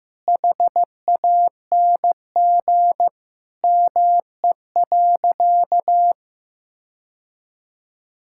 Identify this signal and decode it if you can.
Morse code, heard as a keyed tone, about 15 wpm, 700 Hz. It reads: HANG ME.
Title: Sentences from Top 1600 Words 15wpm